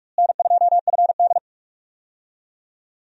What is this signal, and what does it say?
Morse code, heard as a keyed tone, about 45 wpm, 700 Hz. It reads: N2FD
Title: USA Call Signs - 4 Characters - Repeated 3x - 45wpm